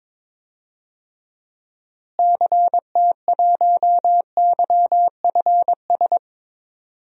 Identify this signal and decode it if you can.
Morse code, heard as a keyed tone, about 22 wpm, 700 Hz. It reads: CT1YFS
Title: DXCC-100 Call Signs - 6 Characters - Repeated 3x - 22wpm